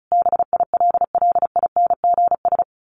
Morse code, heard as a keyed tone, about 34 wpm, 700 Hz. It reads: BILLINGS
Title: A Sound Of Thunder 35wpm